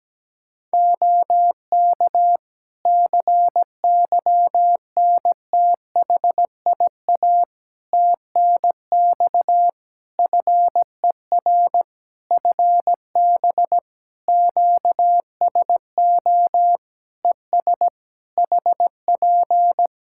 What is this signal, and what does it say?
Morse code, heard as a keyed tone, about 17 wpm, 700 Hz. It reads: OK CYNTHIA TNX FER FB QSO ES HP
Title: K1USN SST QSO Practice 17wpm